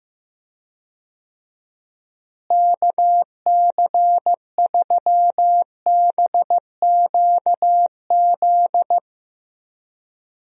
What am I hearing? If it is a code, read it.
Morse code, heard as a keyed tone, about 15 wpm, 700 Hz. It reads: KC3BQZ
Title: USA Call Signs - 6 Characters - Repeated 2x - 15wpm